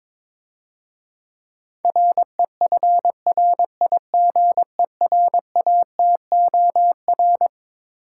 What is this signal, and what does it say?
Morse code, heard as a keyed tone, about 22 wpm, 700 Hz. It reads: REFRIGERATOR
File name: Easy Words - Speed-Racing with Hints - 22wpm